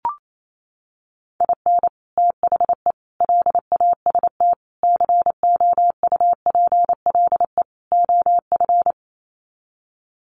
Morse code, heard as a keyed tone, about 28 wpm, 700 Hz. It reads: IN THE LAST COUPLE OF
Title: Sets of 5 Words 28wpm